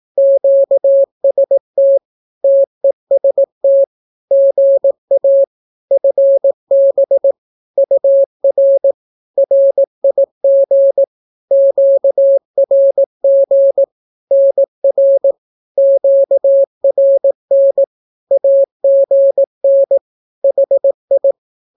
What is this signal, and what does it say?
Morse code, heard as a keyed tone, about 18 wpm, 550 Hz. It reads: QST TEST GA FB UR RIG QRG NR QRN AGN HI
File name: naukapodstaw / skroty / skroty